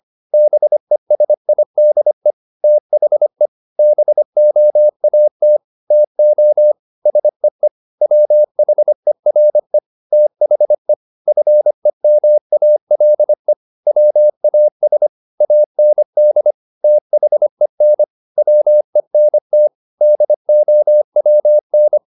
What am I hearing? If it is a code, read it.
Morse code, heard as a keyed tone, about 25 wpm, 600 Hz. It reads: BESIDE THE BOAT TO SEE WHERE THE FEMALE WAS AND THEN WENT DOWN
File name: the old man and the sea part 1 25wpm